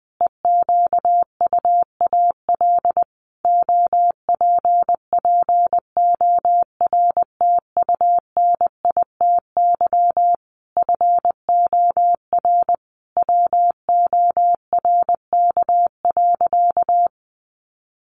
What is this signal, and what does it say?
Morse code, heard as a keyed tone, about 20 wpm, 700 Hz. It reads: EQUAL OPPORTUNITY FOR WORK.
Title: Sentences from Top 1400 Words 20wpm